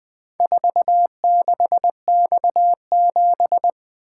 Morse code, heard as a keyed tone, about 20 wpm, 700 Hz. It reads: 46X7
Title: Four Letter-Number Combinations - 20wpm